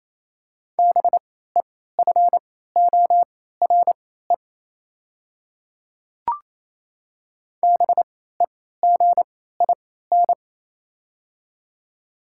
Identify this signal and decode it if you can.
Morse code, heard as a keyed tone, about 28 wpm, spaced as for 16 wpm, 700 Hz. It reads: BEFORE BEGIN
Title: 3x Character Spacing - Top 300 Words - 28wpm